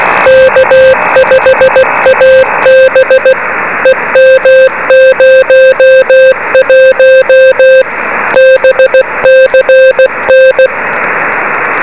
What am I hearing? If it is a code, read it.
Morse code, heard as a keyed tone, about 16 wpm, 520 Hz, with noise. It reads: K5AB EM01 BCN
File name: beacons / k5ab